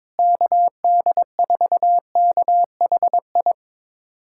Morse code, heard as a keyed tone, about 21 wpm, 700 Hz. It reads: KD4KHI
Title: USA Callsign DE Callsign - 6 Characters - 22wpm